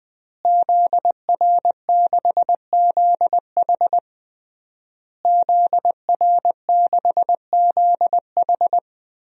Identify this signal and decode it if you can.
Morse code, heard as a keyed tone, about 21 wpm, 700 Hz. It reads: ZR6ZH ZR6ZH
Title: DXCC-100 Call Signs - 5 Characters - Repeated 3x - 20wpm